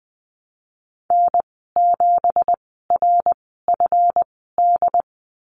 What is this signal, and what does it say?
Morse code, heard as a keyed tone, about 20 wpm, spaced as for 15 wpm, 700 Hz. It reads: N7RFD
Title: 179 Lesson 29g - Slash - Call Signs 20wpm